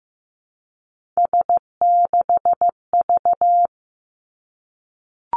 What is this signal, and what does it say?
Morse code, heard as a keyed tone, about 15 wpm, 700 Hz. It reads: S6V
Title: Three Challenging Letter-Number Combinations 15wpm